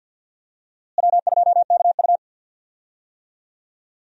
Morse code, heard as a keyed tone, about 52 wpm, 700 Hz. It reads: W2KU
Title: USA Call Signs - 4 Characters - Repeated 3x - 50wpm